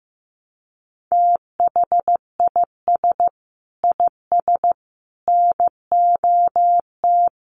Morse code, heard as a keyed tone, about 15 wpm, 700 Hz. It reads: THIS IS NOT
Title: Sentences from Top 1200 Words 15wpm